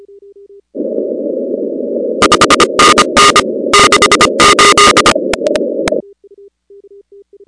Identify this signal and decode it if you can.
Morse code, heard as a keyed tone, about 25 wpm, 1220 Hz, with noise. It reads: 5NN 68
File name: contests / 2019-cw-wpx-kt5j-k5pi- -2330 0-15